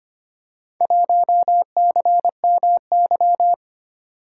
Morse code, heard as a keyed tone, about 25 wpm, 700 Hz. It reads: 1CMY
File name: practice / 25wpm / Four Letter-Number Combinations - 25wpm